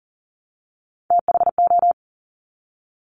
Morse code, heard as a keyed tone, about 42 wpm, 700 Hz. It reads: THO
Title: Top Three Letter Combinations in Words 40wpm